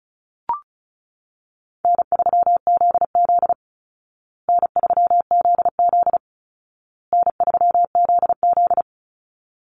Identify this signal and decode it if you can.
Morse code, heard as a keyed tone, about 34 wpm, 700 Hz. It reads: N3ZZ N3ZZ N3ZZ
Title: Easy Call Signs Repeated 3x - 35wpm